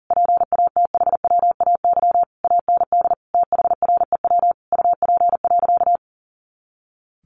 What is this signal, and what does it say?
Morse code, heard as a keyed tone, about 40 wpm, 700 Hz. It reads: PATHWAY AND THREW UP.
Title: A Sound Of Thunder 40wpm